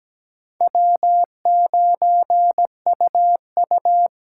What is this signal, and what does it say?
Morse code, heard as a keyed tone, about 17 wpm, 700 Hz. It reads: W9UU
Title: USA Call Signs - 4 Characters - 2x Character Spacing - 17wpm